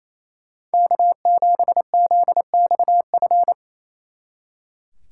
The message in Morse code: K7ZXF

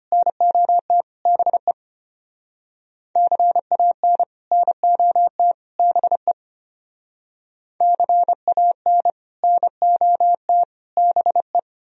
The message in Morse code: NOT BE CAN NOT BE CAN NOT BE

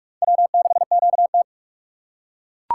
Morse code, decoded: W6QT

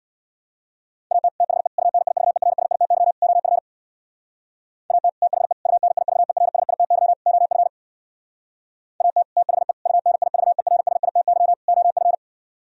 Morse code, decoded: AT THE UNIVERSITY OF AT THE UNIVERSITY OF AT THE UNIVERSITY OF

